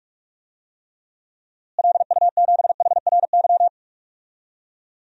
Morse code, heard as a keyed tone, about 46 wpm, 700 Hz. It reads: LU7HRY
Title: DXCC-100 Call Signs - 6 Characters - Repeated 3x - 45wpm